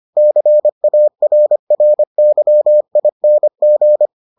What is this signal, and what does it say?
Morse code, heard as a keyed tone, about 25 wpm, 600 Hz. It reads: CARRYING